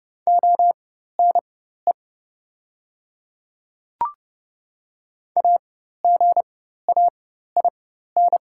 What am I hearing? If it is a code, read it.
Morse code, heard as a keyed tone, about 32 wpm, spaced as for 14 wpm, 690 Hz. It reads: ONE AGAIN
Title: 4x Character Spacing - Top 200 Words - 30wpm